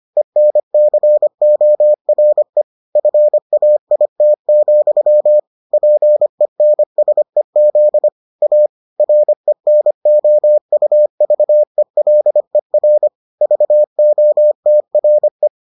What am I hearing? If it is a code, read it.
Morse code, heard as a keyed tone, about 25 wpm, 600 Hz. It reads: ENCORE FAIT, PENSEZ A RENOUVELER VOTRE
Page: bulletin UFT sous forme texte et en CW